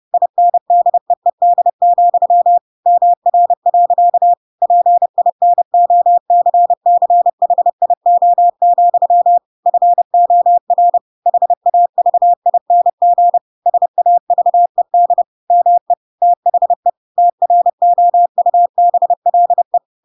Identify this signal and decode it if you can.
Morse code, heard as a keyed tone, about 30 wpm, 700 Hz. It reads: INDEED, MR. PINOCCHIO, FOR HAVING SAVED ME THE TROUBLE
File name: Pinocchio / Pinocchio ch5 30 30 ik1hge